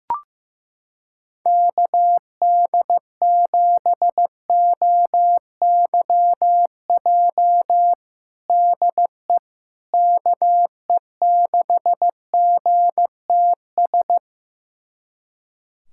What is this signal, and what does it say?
Morse code, heard as a keyed tone, about 15 wpm, 700 Hz. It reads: KD7OYJ DE KE6GTS
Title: USA Callsign DE Callsign - 6 Characters - 15wpm